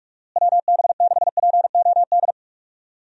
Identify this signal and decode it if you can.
Morse code, heard as a keyed tone, about 44 wpm, 700 Hz. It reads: WB6POD
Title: USA Call Signs - 6 Characters - Repeated 3x - 45wpm